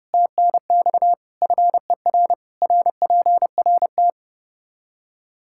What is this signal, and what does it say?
Morse code, heard as a keyed tone, about 30 wpm, 700 Hz. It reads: TNX FER RPRT
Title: Practice QSO - Exchange 2 - Info - 30wpm